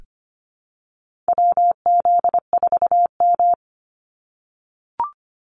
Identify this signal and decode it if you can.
Morse code, heard as a keyed tone, about 25 wpm, 700 Hz. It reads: WZ4M
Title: NA SOTA Activator Callsigns 25wpm